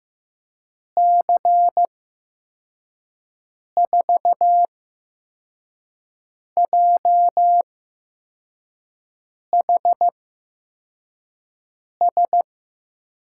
Morse code, heard as a keyed tone, about 15 wpm, spaced as for 4 wpm, 700 Hz. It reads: C4JHS
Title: USA Call Signs - 6 Characters - 8x Character Spacing - 15wpm